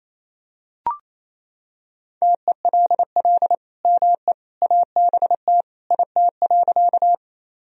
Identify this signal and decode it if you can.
Morse code, heard as a keyed tone, about 28 wpm, 700 Hz. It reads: TELL ME ABT IT.
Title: Sentences from Top 200 Words and Abbreviations 28wpm